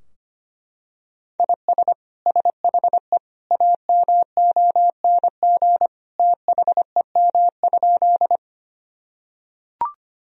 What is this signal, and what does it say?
Morse code, heard as a keyed tone, about 25 wpm, 700 Hz. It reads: IS SHE AMONG THEM?
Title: Sentences from Top 1000 Words - Review Part 3 - 25wpm